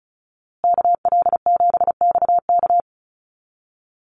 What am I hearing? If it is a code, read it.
Morse code, heard as a keyed tone, about 34 wpm, 700 Hz. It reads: KL7XK
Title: USA Call Signs Repeated 2x 35wpm